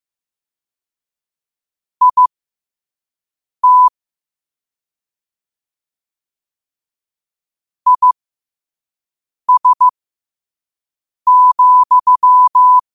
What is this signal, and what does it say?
Morse code, heard as a keyed tone, about 15 wpm, spaced as for 5 wpm, 1000 Hz. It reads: IT IS,
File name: qotdfiles / 5wpm